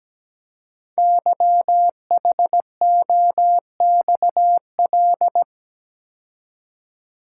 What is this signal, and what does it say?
Morse code, heard as a keyed tone, about 17 wpm, 700 Hz. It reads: YHOXL
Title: Code Groups - 5 Letters - 17wpm